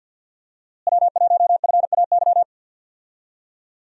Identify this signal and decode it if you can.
Morse code, heard as a keyed tone, about 47 wpm, 700 Hz. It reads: W1FAY
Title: USA Call Signs - 5 Characters - Repeated 2x - 50wpm